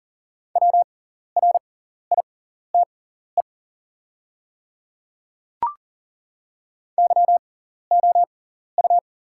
Morse code, heard as a keyed tone, about 38 wpm, spaced as for 14 wpm, 700 Hz. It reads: WRITE YOU
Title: 6x Character Spacing - Top 100 Words - 40wpm